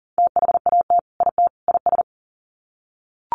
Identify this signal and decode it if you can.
Morse code, heard as a keyed tone, about 40 wpm, 700 Hz. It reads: THAT IT IS
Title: Sets of 3 Words 40wpm